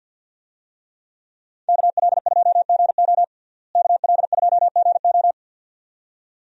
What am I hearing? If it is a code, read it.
Morse code, heard as a keyed tone, about 52 wpm, 700 Hz. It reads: KL2GO KL2GO
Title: USA Call Signs - 5 Characters - Repeated 2x - 50wpm